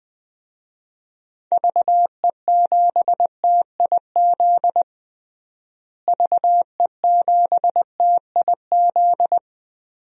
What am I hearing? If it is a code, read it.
Morse code, heard as a keyed tone, about 20 wpm, 700 Hz. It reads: VE7TIZ VE7TIZ
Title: DXCC-100 Call Signs - 6 Characters - Repeated 2x - 20wpm